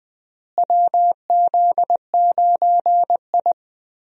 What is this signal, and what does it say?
Morse code, heard as a keyed tone, about 20 wpm, 700 Hz. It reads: WZ9I
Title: USA Call Signs - 4 Characters - Repeated 3x - 20wpm